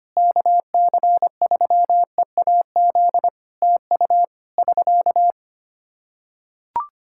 Morse code, heard as a keyed tone, about 25 wpm, 700 Hz. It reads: KC3EAZ TU <SK>
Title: Practice QSO - Exchange 4 - Ending - 25wpm